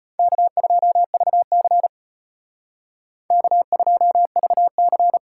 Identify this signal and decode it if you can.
Morse code, heard as a keyed tone, about 30 wpm, 700 Hz. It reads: K2VC K2VC
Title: Speed Racing - USA 1x2 Call Signs 25wpm